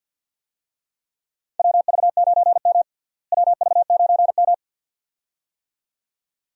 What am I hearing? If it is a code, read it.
Morse code, heard as a keyed tone, about 48 wpm, 700 Hz. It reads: WV9M WV9M
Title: USA Call Signs - 4 Characters - Repeated 2x - 50wpm